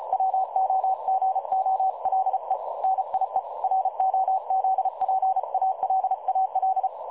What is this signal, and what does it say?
Morse code, heard as a keyed tone, about 33 wpm, 760 Hz. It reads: WOGOLE NIE ROZPALAL
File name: dlazaawansowanych / nagraniazpasma / websdr recording 3565.0kHz